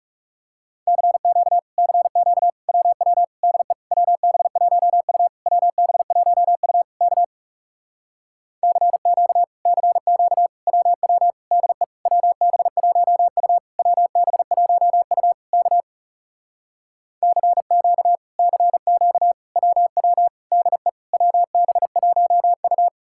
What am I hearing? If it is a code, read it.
Morse code, heard as a keyed tone, about 37 wpm, 700 Hz. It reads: CQ CQ WW DE WB1U WB1U K CQ CQ WW DE WB1U WB1U K CQ CQ WW DE WB1U
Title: Speed-Racing Common QSO Phrases - 30wpm